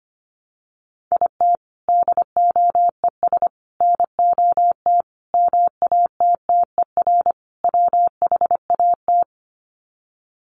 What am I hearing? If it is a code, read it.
Morse code, heard as a keyed tone, about 25 wpm, 700 Hz. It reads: IT DOES NOT MATTER WHAT
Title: Sets of 5 Words Encore 25wpm